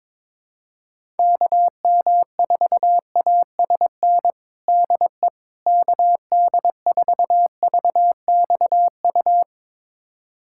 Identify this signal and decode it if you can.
Morse code, heard as a keyed tone, about 22 wpm, 700 Hz. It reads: KM4ASN DE KD4VXU
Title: USA Callsign DE Callsign - 6 Characters - 22wpm